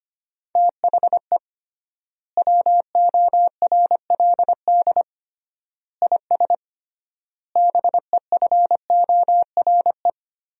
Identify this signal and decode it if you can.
Morse code, heard as a keyed tone, about 25 wpm, 700 Hz. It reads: THE WORLD IS BEFORE
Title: 3x Word-Spacing - Sentences from Top 200 Words - 25wpm